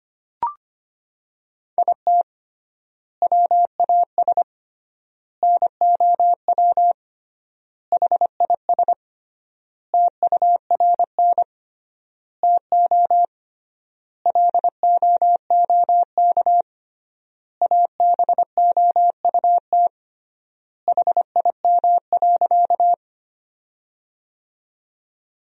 Morse code, heard as a keyed tone, about 25 wpm, 700 Hz. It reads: IT WAS NOW HIS TURN TO LOOK ABOUT HIM.
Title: 3x Word-Spacing - Sentences from Top 200 Words - 25wpm